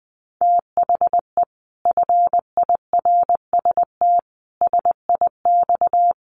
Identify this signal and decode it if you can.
Morse code, heard as a keyed tone, about 19 wpm, 700 Hz. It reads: THE FIRST SIX
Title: Sets of 5 Words Encore2 20wpm